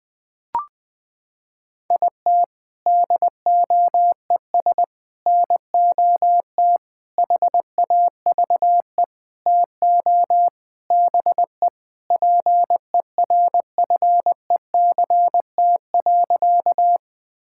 Morse code, heard as a keyed tone, about 19 wpm, 700 Hz. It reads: IT DOES NOT HAVE TO BE PERFECT.
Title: Sentences with 5-Word Phrases - 20wpm